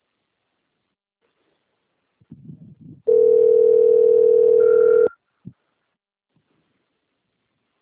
Ringback tone (North America), 440 + 480 Hz.